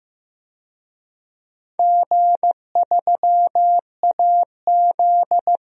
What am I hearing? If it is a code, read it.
Morse code, heard as a keyed tone, about 15 wpm, 700 Hz. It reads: G3AZ
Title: DXCC-100 Call Signs - 4 Characters - 15wpm